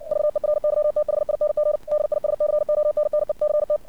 Morse code, heard as a keyed tone, about 56 wpm, 600 Hz. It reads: VERYTHING DISCONNECT
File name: archived / CRSnet